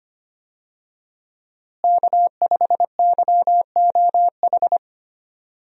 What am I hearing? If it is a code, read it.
Morse code, heard as a keyed tone, about 25 wpm, 700 Hz. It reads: K5YOH
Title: USA Call Signs - 5 Characters - Repeated 2x - 25wpm